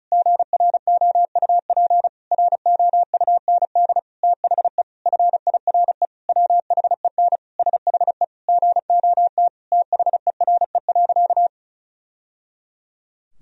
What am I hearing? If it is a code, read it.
Morse code, heard as a keyed tone, about 36 wpm, 700 Hz. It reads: GROUP ROUND THE FIRE WHEN SHE GOT THERE.
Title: Sentences from Top 1000 Words - Review Part 1 - 35wpm